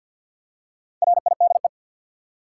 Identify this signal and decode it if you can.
Morse code, heard as a keyed tone, about 52 wpm, 700 Hz. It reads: RIDE
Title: Four Letter Words - 50wpm